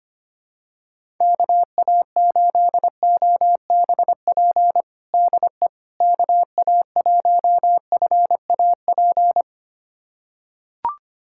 Morse code, heard as a keyed tone, about 25 wpm, 700 Hz. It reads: KA8OBP DE KA1FAP
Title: USA Callsign DE Callsign - 6 Characters - 25wpm